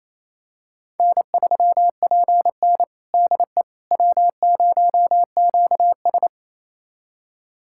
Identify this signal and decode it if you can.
Morse code, heard as a keyed tone, about 28 wpm, 700 Hz. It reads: N3PN DE W0QS
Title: USA Callsign DE Callsign - 4 Characters - 28wpm